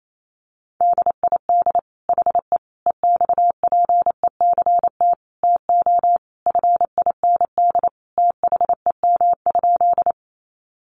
Morse code, heard as a keyed tone, about 28 wpm, 700 Hz. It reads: DID HE EXPECT TO FIND THEM?
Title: Sentences from Top 1100 Words 28wpm